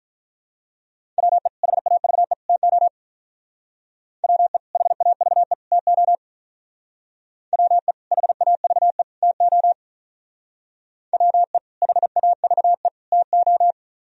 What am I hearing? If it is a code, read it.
Morse code, heard as a keyed tone, about 44 wpm, 700 Hz. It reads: WE HAVE TO WE HAVE TO WE HAVE TO WE HAVE TO